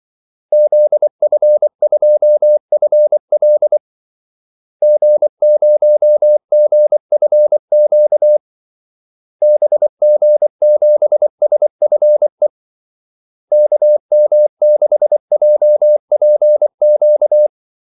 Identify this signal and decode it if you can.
Morse code, heard as a keyed tone, about 24 wpm, 600 Hz. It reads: ZF2FL G0GFQ BG7SFE KM6JPQ
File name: Callsigns-24wpm